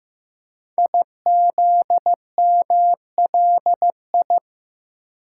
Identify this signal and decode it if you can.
Morse code, heard as a keyed tone, about 15 wpm, 700 Hz. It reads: IZMLI
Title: Code Groups - 5 Letters - 15wpm